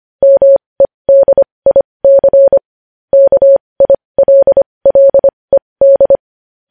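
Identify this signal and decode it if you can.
Morse code, heard as a keyed tone, about 25 wpm, spaced as for 20 wpm, 550 Hz. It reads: MEDIC KILLED
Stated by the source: Morse practice of news headlines at 20 wpm.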